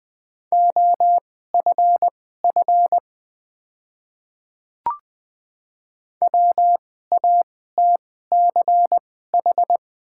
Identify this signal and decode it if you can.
Morse code, heard as a keyed tone, about 21 wpm, spaced as for 15 wpm, 700 Hz. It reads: OFF WATCH
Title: 2x Character Spacing - Top 300 Words - 20wpm